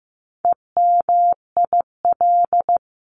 Morse code, heard as a keyed tone, about 15 wpm, 700 Hz. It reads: EMIL
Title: CWT Exchange 15wpm